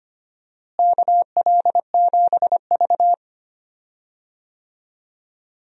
Morse code, heard as a keyed tone, about 25 wpm, 700 Hz. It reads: KL7V
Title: DXCC-100 Call Signs - 4 Characters - Repeated 2x - 25wpm